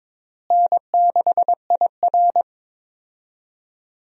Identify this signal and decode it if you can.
Morse code, heard as a keyed tone, about 22 wpm, 700 Hz. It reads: N6IR
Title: USA Call Signs - 4 Characters - Repeated 2x - 22wpm